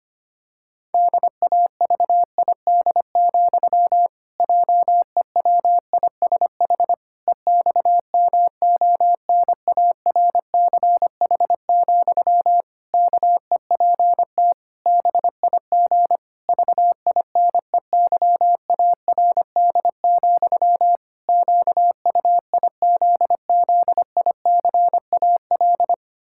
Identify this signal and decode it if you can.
Morse code, heard as a keyed tone, about 25 wpm, 700 Hz. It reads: DAVID, JEWISH EXMONARCH, KEPT BIG VINEYARD, QUIZZICAL
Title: Pangram Challenge v2 - 25wpm